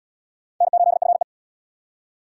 Morse code, heard as a keyed tone, about 76 wpm, 700 Hz. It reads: N8FE